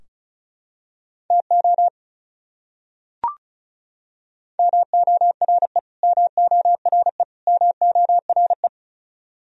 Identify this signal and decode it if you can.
Morse code, heard as a keyed tone, about 34 wpm, 700 Hz. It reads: TO MORE MORE MORE
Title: Top 100 Words Repeated 35wpm